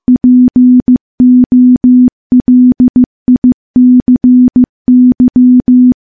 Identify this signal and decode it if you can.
Morse code, heard as a keyed tone, about 15 wpm, 250 Hz. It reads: POLICY
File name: morse-training-15WPM